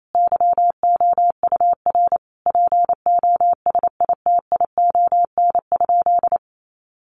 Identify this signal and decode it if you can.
Morse code, heard as a keyed tone, about 28 wpm, 700 Hz. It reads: YOUR POSITION?
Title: Sentences from Top 1400 Words 28wpm